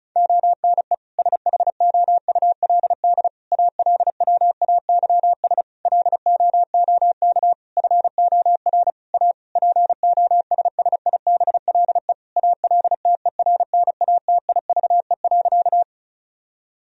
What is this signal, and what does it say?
Morse code, heard as a keyed tone, about 36 wpm, 700 Hz. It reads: ONE SHOULD ALWAYS LOOK FOR A POSSIBLE ALTERNATIVE.
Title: Sentences from Top 1000 Words - Review Part 4 - 35wpm